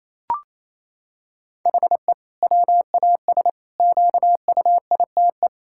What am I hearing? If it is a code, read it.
Morse code, heard as a keyed tone, about 28 wpm, 700 Hz. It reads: HE WAS QUITE
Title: Sentences from Top 1000 Words - Review Part 5 - 28wpm